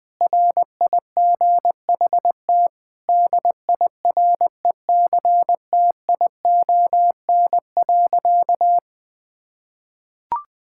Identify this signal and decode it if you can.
Morse code, heard as a keyed tone, about 21 wpm, 700 Hz. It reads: RIGHT DIRECTION.
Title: Sentences with 7-Word Phrases - Spoken Beginning - 20wpm